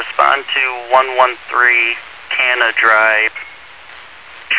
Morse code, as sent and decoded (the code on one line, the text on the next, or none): none